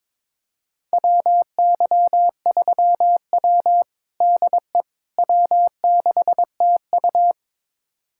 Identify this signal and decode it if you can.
Morse code, heard as a keyed tone, about 22 wpm, 700 Hz. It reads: WY3W DE W6TU
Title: USA Callsign DE Callsign - 4 Characters - 22wpm